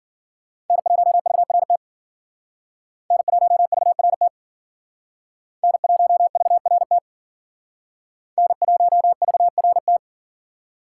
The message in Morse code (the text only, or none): N1VRT N1VRT N1VRT N1VRT